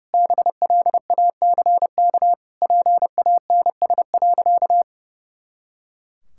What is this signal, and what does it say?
Morse code, heard as a keyed tone, about 30 wpm, 700 Hz. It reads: BLACK PANS.
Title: Pangram Challenge v2 - 30wpm